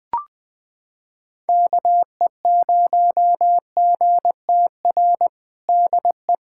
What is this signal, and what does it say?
Morse code, heard as a keyed tone, about 20 wpm, 700 Hz. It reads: KE0GTR DE
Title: USA Callsign DE Callsign - 6 Characters - 20wpm